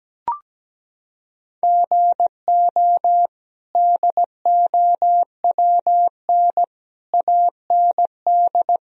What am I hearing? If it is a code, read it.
Morse code, heard as a keyed tone, about 17 wpm, 700 Hz. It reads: GO DOWN AND
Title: Sentences from Top 1000 Words - Review Part 1 - 17wpm